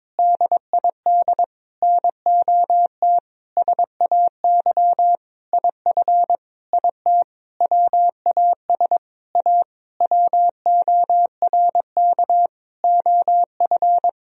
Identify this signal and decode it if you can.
Morse code, heard as a keyed tone, about 21 wpm, 700 Hz. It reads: DID NOT SAY IF IT WAS A WORK OF
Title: Sentences from Top 1000 Words - Review Part 3 - 22wpm